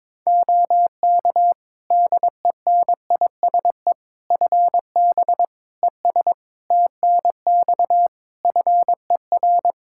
Morse code, heard as a keyed tone, about 22 wpm, 700 Hz. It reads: OK DENISE FB ES TNX FER
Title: Practice QSO - Exchange 2 - Info - 22wpm